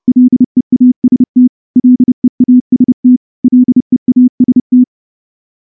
Morse code, heard as a keyed tone, about 30 wpm, 250 Hz. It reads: LEAST LEAST LEAST
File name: morse-training-30WPM